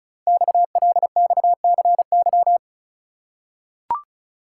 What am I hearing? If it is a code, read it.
Morse code, heard as a keyed tone, about 36 wpm, 700 Hz. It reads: XLXCY
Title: Code Groups - 5 Letters - 35wpm